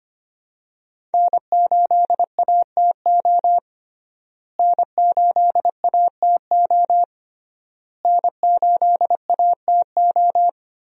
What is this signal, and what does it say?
Morse code, heard as a keyed tone, about 25 wpm, 700 Hz. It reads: N8ATO N8ATO N8ATO
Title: USA Call Signs - 5 Characters - Repeated 3x - 25wpm